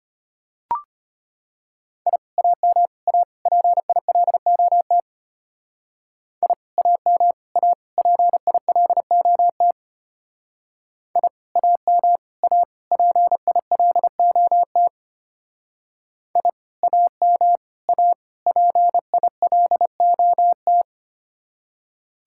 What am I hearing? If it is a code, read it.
Morse code, heard as a keyed tone, about 32 wpm, 700 Hz. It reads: I AM A PILOT I AM A PILOT I AM A PILOT I AM A PILOT